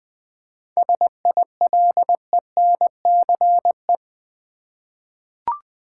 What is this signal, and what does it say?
Morse code, heard as a keyed tone, about 20 wpm, 700 Hz. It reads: SILENCE
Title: 7 Letter Words - First 3 Letters Given - 20wpm